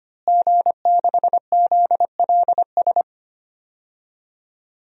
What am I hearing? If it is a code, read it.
Morse code, heard as a keyed tone, about 25 wpm, 700 Hz. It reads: G6ZLS
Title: DXCC-100 Call Signs - 5 Characters - Repeated 3x - 25wpm